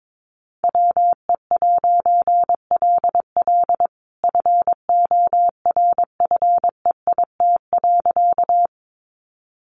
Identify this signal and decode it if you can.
Morse code, heard as a keyed tone, about 22 wpm, 700 Hz. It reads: WE'LL FORFEIT.
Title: A Sound Of Thunder 22wpm